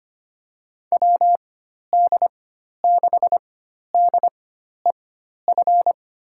Morse code, heard as a keyed tone, about 25 wpm, spaced as for 12 wpm, 700 Hz. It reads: WD6DEF
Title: USA Call Signs - 6 Characters - 4x Character Spacing - 25wpm